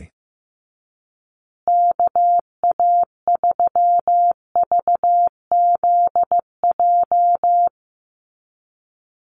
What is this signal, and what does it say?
Morse code, heard as a keyed tone, about 15 wpm, 700 Hz. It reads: KA3VZJ
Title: USA Call Signs Repeated 3x 15wpm